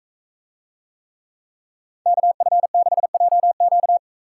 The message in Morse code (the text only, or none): KF6JQ